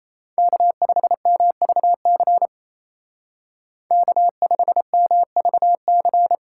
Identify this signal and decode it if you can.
Morse code, heard as a keyed tone, about 33 wpm, 700 Hz. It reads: K5MVC K5MVC
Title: Speed Racing - USA Call Signs 28wpm